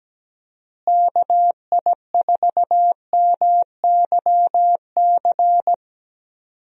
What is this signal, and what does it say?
Morse code, heard as a keyed tone, about 17 wpm, 700 Hz. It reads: KI4MYC
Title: USA Call Signs - 6 Characters - Repeated 3x - 17wpm